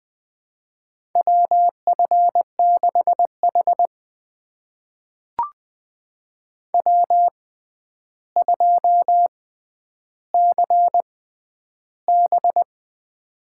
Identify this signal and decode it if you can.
Morse code, heard as a keyed tone, about 21 wpm, spaced as for 10 wpm, 700 Hz. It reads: WF6H W2CB
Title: USA Call Signs - 6x Character Spacing - 20wpm